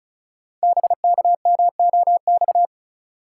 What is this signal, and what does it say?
Morse code, heard as a keyed tone, about 36 wpm, 700 Hz. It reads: BKMOX
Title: Code Groups - 5 Letters - 35wpm